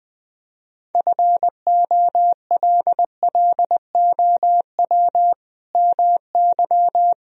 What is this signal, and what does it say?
Morse code, heard as a keyed tone, about 20 wpm, 700 Hz. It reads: FOLLOW MY
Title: Sentences from Top 1000 Words - Review Part 1 - 20wpm